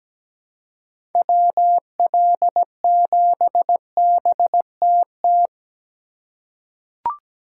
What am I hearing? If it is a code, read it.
Morse code, heard as a keyed tone, about 17 wpm, 700 Hz. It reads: WL7BTT
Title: USA Call Signs - 6 Characters - 3x Character Spacing - 17wpm